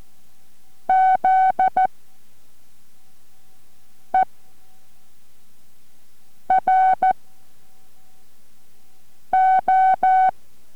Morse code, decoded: ZERO